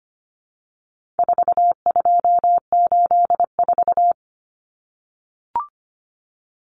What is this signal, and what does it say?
Morse code, heard as a keyed tone, about 25 wpm, 700 Hz. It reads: 4284
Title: Common Cut Numbers - 4-Digit 25wpm